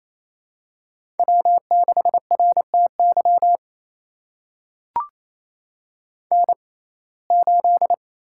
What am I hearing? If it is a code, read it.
Morse code, heard as a keyed tone, about 28 wpm, spaced as for 20 wpm, 700 Hz. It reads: W6RTY N8
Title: USA Call Signs - 5 Characters - 6x Character Spacing - 28wpm